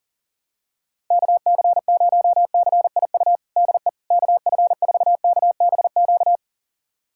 Morse code, heard as a keyed tone, about 40 wpm, 700 Hz. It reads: KC0CIU DE KF4KBQ
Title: USA Callsign DE Callsign - 6 Characters - 40wpm